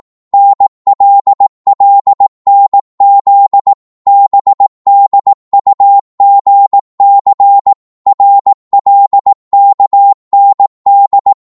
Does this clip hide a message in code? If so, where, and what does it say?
Yes: Morse code from 0.3 to 11.3 s, heard as a keyed tone, about 18 wpm, 800 Hz. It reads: NLLNZ BDUGC RLKND